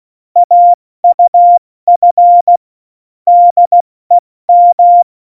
Morse code, heard as a keyed tone, about 16 wpm, spaced as for 14 wpm, 700 Hz. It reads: AUF DEM
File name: Morsekurs / Witze / Witz 06. November 2018, 70 F80